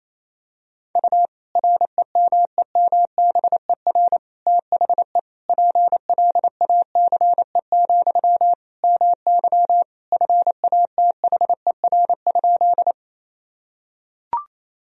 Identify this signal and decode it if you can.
Morse code, heard as a keyed tone, about 27 wpm, 700 Hz. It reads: U REMEMBER THE PLACE, MY FATHER?
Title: Sentences from Top 400 Words and Abbreviations 28wpm